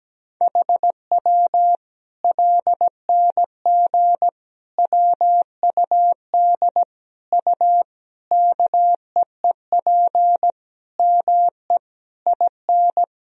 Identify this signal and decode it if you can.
Morse code, heard as a keyed tone, about 17 wpm, 700 Hz. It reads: HW LNG WUD U KEEP ME IN
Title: Sentences from Top 500 Words and Abbreviations 17wpm